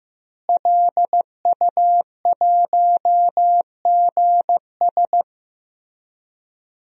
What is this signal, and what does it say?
Morse code, heard as a keyed tone, about 16 wpm, 700 Hz. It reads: LU1GS
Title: DXCC-100 Call Signs - 5 Characters - Repeated 3x - 15wpm